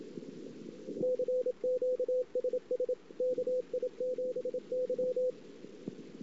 Morse code, heard as a keyed tone, about 26 wpm, 500 Hz, with radio noise. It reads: CQ SS KI7Y
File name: ss-cw